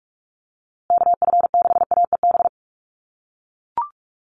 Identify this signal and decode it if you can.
Morse code, heard as a keyed tone, about 44 wpm, 700 Hz. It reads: KF6AEB
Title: USA Call Signs Repeated 3x 45wpm